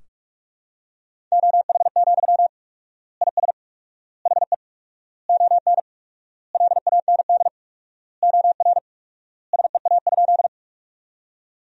Morse code, heard as a keyed tone, about 46 wpm, 700 Hz. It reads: OH, IS HE ON LAND OR SEA?
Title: 4x Word-Spacing - Sentences from Top 500 Words - 45wpm